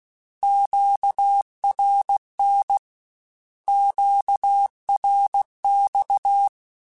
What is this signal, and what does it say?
Morse code, heard as a keyed tone, about 16 wpm, 800 Hz. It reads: QRN QRX
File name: Short Runs / Short Runs-mp3s / Words @15